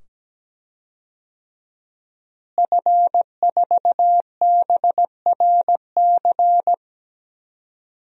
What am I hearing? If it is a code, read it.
Morse code, heard as a keyed tone, about 17 wpm, 700 Hz. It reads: F4BRC
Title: DXCC-100 Call Signs - 5 Characters - Repeated 2x - 17wpm